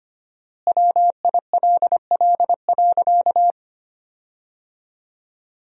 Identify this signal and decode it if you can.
Morse code, heard as a keyed tone, about 25 wpm, 700 Hz. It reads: WILL.
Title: 4x Word-Spacing - Sentences from Top 100 Words - 25wpm